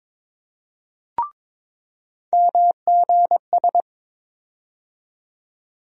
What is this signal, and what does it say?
Morse code, heard as a keyed tone, about 22 wpm, 700 Hz. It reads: MGS
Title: USA Call Sign Postfixes - 3 Letters 22wpm